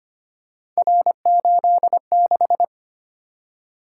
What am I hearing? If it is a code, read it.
Morse code, heard as a keyed tone, about 25 wpm, 700 Hz. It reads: R86
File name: Three Letter-Number Combinations - 25wpm